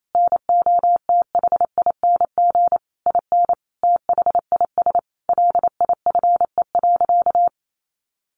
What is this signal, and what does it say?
Morse code, heard as a keyed tone, about 28 wpm, 700 Hz. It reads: NOTHING IN THIS LIFE.
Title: Sentences from Top 1200 Words 28wpm